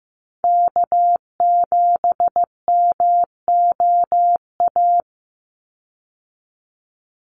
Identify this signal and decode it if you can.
Morse code, heard as a keyed tone, about 15 wpm, 700 Hz. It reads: K7MOA
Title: Call Signs Repeated 3x - 15wpm